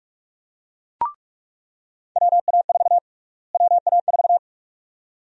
Morse code, heard as a keyed tone, about 46 wpm, 700 Hz. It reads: WA4 WA4
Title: USA Call Sign Prefixes - Repeated 2x - 45wpm